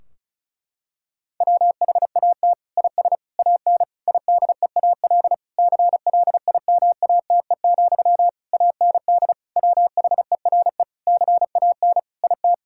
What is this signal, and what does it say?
Morse code, heard as a keyed tone, about 34 wpm, 700 Hz. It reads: WHAT IS AN IDEAL CLIMATE, AND WHERE CAN IT
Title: Sentences from Top 1600 Words 35wpm